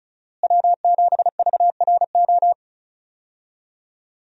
Morse code, heard as a keyed tone, about 36 wpm, 700 Hz. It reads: W7VRO
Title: USA Call Signs - 5 Characters - Repeated 3x - 35wpm